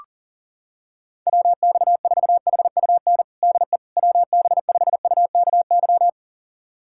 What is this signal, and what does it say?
Morse code, heard as a keyed tone, about 40 wpm, 700 Hz. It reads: WX4HUN DE WB5UKY
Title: USA Callsign DE Callsign - 6 Characters - 40wpm